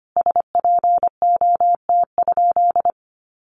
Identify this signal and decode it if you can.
Morse code, heard as a keyed tone, about 25 wpm, 700 Hz. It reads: SPOT?
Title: Sentences from Top 1100 Words 25wpm